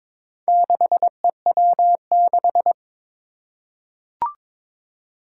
Morse code, heard as a keyed tone, about 22 wpm, 700 Hz. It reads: 6EW6
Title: Four Challenging Letter-Number Combinations 22wpm